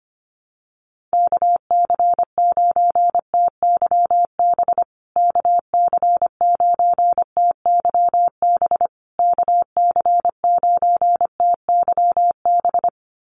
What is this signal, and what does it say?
Morse code, heard as a keyed tone, about 25 wpm, 700 Hz. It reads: KC9TYB KC9TYB KC9TYB
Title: USA Call Signs Repeated 3x 25wpm